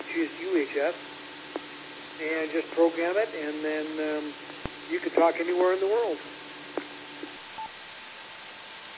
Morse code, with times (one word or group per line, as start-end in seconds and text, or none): none